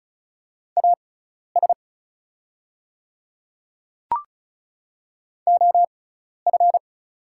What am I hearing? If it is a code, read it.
Morse code, heard as a keyed tone, about 36 wpm, spaced as for 12 wpm, 690 Hz. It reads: AS OF
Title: 6x Character Spacing - Top 100 Words - 35wpm